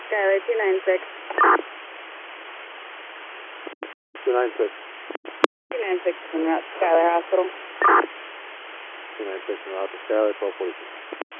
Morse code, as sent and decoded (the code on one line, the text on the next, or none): none